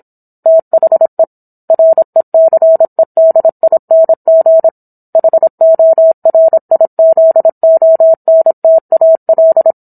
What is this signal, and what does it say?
Morse code, heard as a keyed tone, about 27 wpm, 650 Hz. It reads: THE RECEDING HORIZONTAL